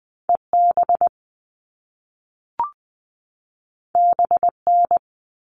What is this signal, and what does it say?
Morse code, heard as a keyed tone, about 20 wpm, 700 Hz. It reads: EB BN
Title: All Cut Numbers - 2-Digit 20wpm